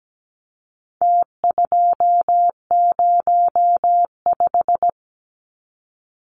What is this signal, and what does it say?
Morse code, heard as a keyed tone, about 17 wpm, 700 Hz. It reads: T205
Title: Numbers - Contest Style - 4-Digit 17wpm